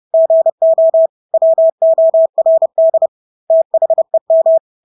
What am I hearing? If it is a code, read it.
Morse code, heard as a keyed tone, about 30 wpm, 650 Hz. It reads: GO WORD THEM
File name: Engelske ord 30-30 wpm